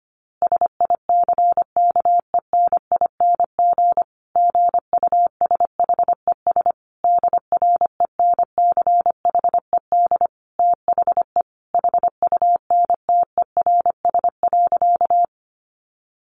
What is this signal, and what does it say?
Morse code, heard as a keyed tone, about 25 wpm, 700 Hz. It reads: SICKENING GUSHES DRENCHED THE HUNTERS.
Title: A Sound Of Thunder 25wpm